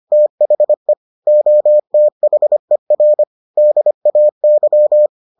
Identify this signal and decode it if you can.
Morse code, heard as a keyed tone, about 25 wpm, 600 Hz. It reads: THE OTHER DAY